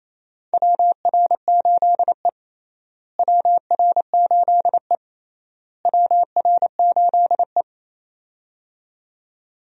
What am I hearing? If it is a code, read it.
Morse code, heard as a keyed tone, about 28 wpm, 700 Hz. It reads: WR8E WR8E WR8E
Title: USA Call Signs - 4 Characters - Repeated 3x - 28wpm